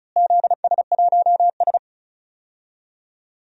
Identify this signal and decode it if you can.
Morse code, heard as a keyed tone, about 34 wpm, 700 Hz. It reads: ZS1S
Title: DXCC-100 Call Signs - 4 Characters - Repeated 2x - 35wpm